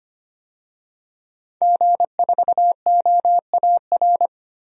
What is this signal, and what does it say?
Morse code, heard as a keyed tone, about 25 wpm, 700 Hz. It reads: G4OAR
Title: DXCC-100 Call Signs - 5 Characters - 25wpm